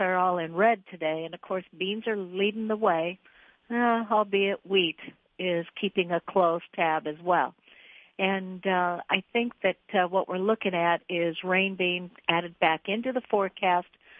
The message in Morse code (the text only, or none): none